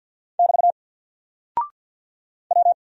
Morse code, heard as a keyed tone, about 48 wpm, 700 Hz. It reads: <BT> W
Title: Single Character (Letter:Number:Punctuation:Prosign) Alt 50wpm